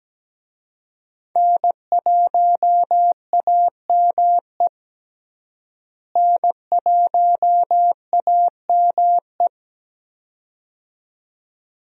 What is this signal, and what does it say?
Morse code, heard as a keyed tone, about 17 wpm, 700 Hz. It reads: N1AME N1AME
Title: USA Call Signs - 5 Characters - Repeated 2x - 17wpm